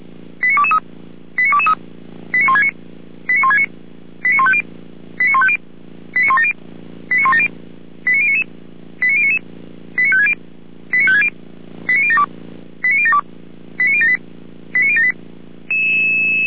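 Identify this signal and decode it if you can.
Five-tone selective call (ZVEI), heard as quick runs of five tones, tones of about 70 ms. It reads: 89223 89223 89169 89169 89150 89150 89170 89170 89099 89099 89560 89560 89072 89072 89078 89078
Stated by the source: ZVEI five-tone selective call received with an SDR, archived by the Feuerwehr Orlamünde fire brigade.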